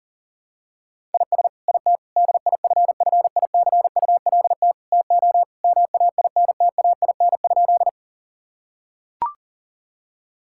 Morse code, heard as a keyed tone, about 40 wpm, 700 Hz. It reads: IS IT DIFFICULT TO MAINTAIN?
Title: Sentences from Top 1000 Words - Review Part 5 - 40wpm